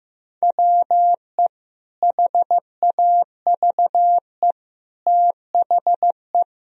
Morse code, heard as a keyed tone, about 15 wpm, 700 Hz. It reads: WE HAVE THE
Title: Sentences from Top 1000 Words - Review Part 3 - 15wpm